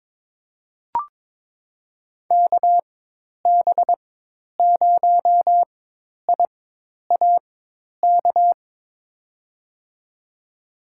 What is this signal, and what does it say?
Morse code, heard as a keyed tone, about 22 wpm, spaced as for 10 wpm, 700 Hz. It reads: KB0IAK
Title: USA Call Signs - 6 Characters - 4x Character Spacing - 22wpm